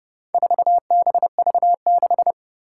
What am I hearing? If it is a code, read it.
Morse code, heard as a keyed tone, about 30 wpm, 700 Hz. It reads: 4BV6
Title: Four Challenging Letter-Number Combinations 30wpm